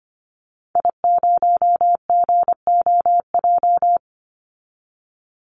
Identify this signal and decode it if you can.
Morse code, heard as a keyed tone, about 25 wpm, 700 Hz. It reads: I0GOJ
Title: Call Signs Repeated 2x - 25wpm